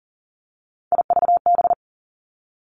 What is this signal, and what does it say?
Morse code, heard as a keyed tone, about 39 wpm, 700 Hz. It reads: IVB
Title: Three Challenging Letter-Number Combinations 40wpm